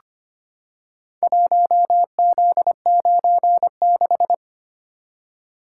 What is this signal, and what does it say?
Morse code, heard as a keyed tone, about 25 wpm, 700 Hz. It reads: 1Z96
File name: practice / 25wpm / Four Letter-Number Combinations - 25wpm